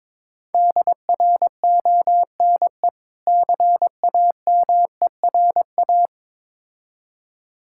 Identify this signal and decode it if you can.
Morse code, heard as a keyed tone, about 22 wpm, 690 Hz. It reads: DRONE CAMERA
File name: Difficult Words with Hints - 22wpm